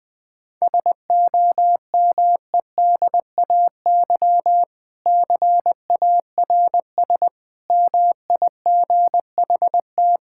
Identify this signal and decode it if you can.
Morse code, heard as a keyed tone, about 20 wpm, 700 Hz. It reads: SOMEDAY CARS MIGHT
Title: Sentences with 6-Word Phrases - 20wpm